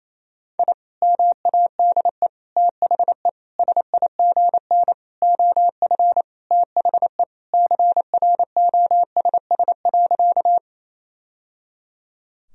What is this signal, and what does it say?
Morse code, heard as a keyed tone, about 28 wpm, 700 Hz. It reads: I MADE THE SIGN OF THE CROSS.
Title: Sentences from Top 1000 Words - Review Part 1 - 28wpm